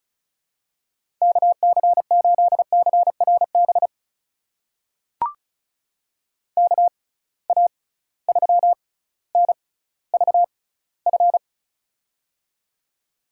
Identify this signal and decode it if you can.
Morse code, heard as a keyed tone, about 34 wpm, spaced as for 18 wpm, 700 Hz. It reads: KC8CRB KA3NVF
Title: USA Call Signs - 6 Characters - 6x Character Spacing - 35wpm